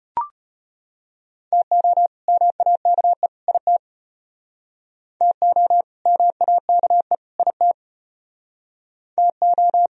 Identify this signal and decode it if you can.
Morse code, heard as a keyed tone, about 34 wpm, 700 Hz. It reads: TO MAKE IT TO MAKE IT TO